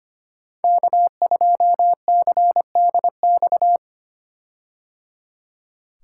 Morse code, heard as a keyed tone, about 25 wpm, 700 Hz. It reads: K2CDX
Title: K1USN SST QSO Practice - 25wpm